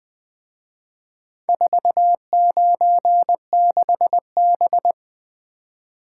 Morse code, heard as a keyed tone, about 22 wpm, 700 Hz. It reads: 496B
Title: Speed-Racing - Four Letter-Number Combinations - 20wpm